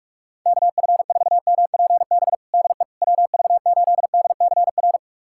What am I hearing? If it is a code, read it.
Morse code, heard as a keyed tone, about 41 wpm, 700 Hz. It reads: KF4MPB DE WV8DCR
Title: USA Callsign DE Callsign 45wpm